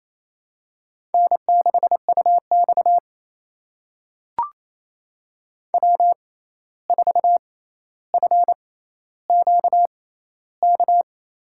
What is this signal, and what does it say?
Morse code, heard as a keyed tone, about 28 wpm, spaced as for 13 wpm, 700 Hz. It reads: N6UX W4FQK
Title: USA Call Signs - 6x Character Spacing - 28wpm